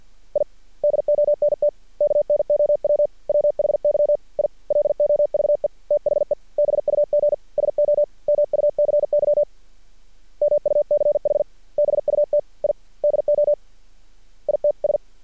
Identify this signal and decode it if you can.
Morse code, heard as a keyed tone, about 50 wpm, 590 Hz. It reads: I DONT KNOW WHY I LOVE THE BUG SO MUCY MUCH BUT I DO ITS
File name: archived / CRSnet